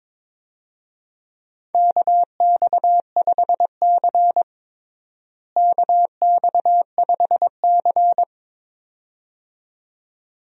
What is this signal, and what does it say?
Morse code, heard as a keyed tone, about 22 wpm, 700 Hz. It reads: KX5C KX5C
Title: USA Call Signs - 4 Characters - Repeated 2x - 22wpm